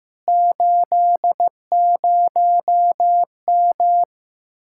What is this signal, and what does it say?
Morse code, heard as a keyed tone, about 16 wpm, 700 Hz. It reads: 80M
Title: Speed-Racing Common QSO Phrases Encore - 15wpm